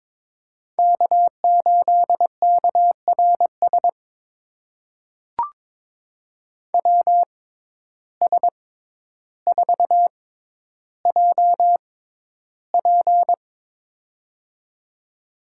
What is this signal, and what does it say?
Morse code, heard as a keyed tone, about 22 wpm, spaced as for 11 wpm, 700 Hz. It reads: K8KRS WS4JP
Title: USA Call Signs - 5 Characters - 6x Character Spacing - 22wpm